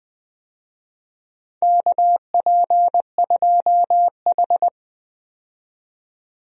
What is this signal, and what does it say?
Morse code, heard as a keyed tone, about 20 wpm, 700 Hz. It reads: KP2H
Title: DXCC-100 Call Signs - 4 Characters - 20wpm